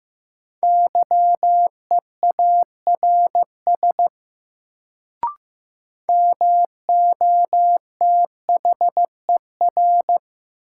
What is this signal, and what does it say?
Morse code, heard as a keyed tone, about 15 wpm, 700 Hz. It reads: YEARS MOTHER
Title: Sentences from Top 200 Words - New Word Drill - 15wpm